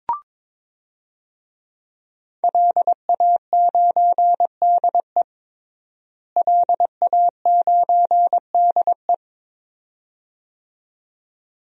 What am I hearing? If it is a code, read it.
Morse code, heard as a keyed tone, about 22 wpm, 700 Hz. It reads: LA9DE LA9DE
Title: DXCC-100 Call Signs - 5 Characters - Repeated 2x - 22wpm